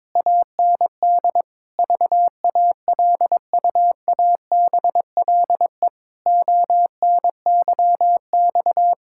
Morse code, heard as a keyed tone, about 22 wpm, 700 Hz. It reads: AND VALUABLE ONYX
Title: Pangram Challenge v2 - 22wpm